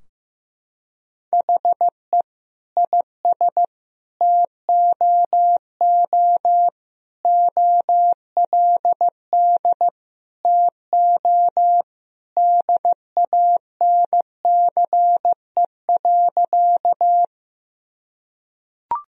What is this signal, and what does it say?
Morse code, heard as a keyed tone, about 15 wpm, 700 Hz. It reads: HE IS TOO OLD TO DANCE.
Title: Sentences from Top 1000 Words - Review Part 4 - 15wpm